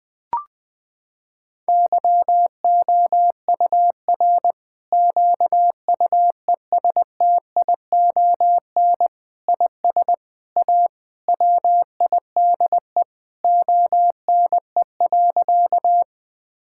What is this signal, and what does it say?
Morse code, heard as a keyed tone, about 21 wpm, 700 Hz. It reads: YOUR QUESTION IS A WIDE ONE.
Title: Sentences from Top 1000 Words - Review Part 6 - 20wpm